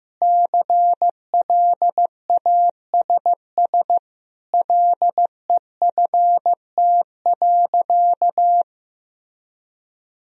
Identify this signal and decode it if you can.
Morse code, heard as a keyed tone, about 16 wpm, 700 Hz. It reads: CLASS LEFT.
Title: Sentences from Top 1000 Words - Review Part 2 - 15wpm